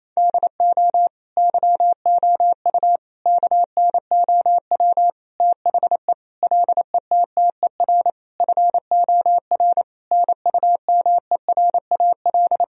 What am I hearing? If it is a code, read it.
Morse code, heard as a keyed tone, about 27 wpm, 700 Hz. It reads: DO YOU KNOW THE LETTER FOR NUMERAL
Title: Sentences from Top 1000 Words - Review Part 2 - 28wpm